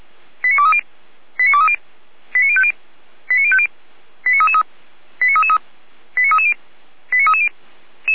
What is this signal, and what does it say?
Five-tone selective call (ZVEI), heard as quick runs of five tones, tones of about 70 ms. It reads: 89230 89230 89050 89050 89333 89333 89330 89330
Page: /455/SDR/ZVEI/89320